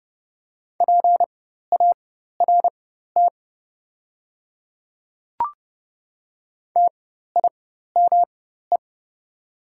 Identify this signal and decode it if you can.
Morse code, heard as a keyed tone, about 30 wpm, spaced as for 14 wpm, 700 Hz. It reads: PART TIME
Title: 4x Character Spacing - Top 200 Words - 30wpm